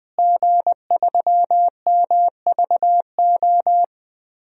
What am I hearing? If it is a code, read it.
Morse code, heard as a keyed tone, about 20 wpm, 700 Hz. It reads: G3MVO
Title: DXCC-100 Call Signs - 5 Characters - Repeated 2x - 20wpm